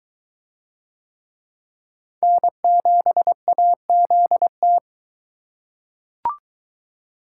Morse code, heard as a keyed tone, about 22 wpm, 700 Hz. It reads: N7AZT